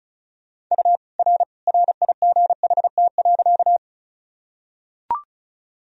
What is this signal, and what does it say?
Morse code, heard as a keyed tone, about 36 wpm, 700 Hz. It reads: U R RIGHT.
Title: Sentences from Top 200 Words and Abbreviations 35wpm